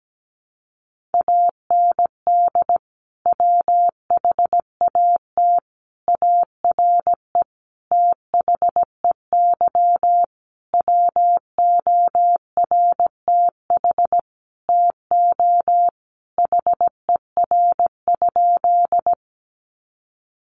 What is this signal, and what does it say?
Morse code, heard as a keyed tone, about 17 wpm, 700 Hz. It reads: AND WHAT ARE THEY WORTH TO HER?
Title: Sentences from Top 800 Words 17wpm